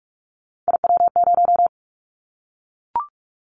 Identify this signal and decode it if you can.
Morse code, heard as a keyed tone, about 44 wpm, 700 Hz. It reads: IW0
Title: Callsign Prefixes for Top 25 DXCC - Repeated 3x - 45wpm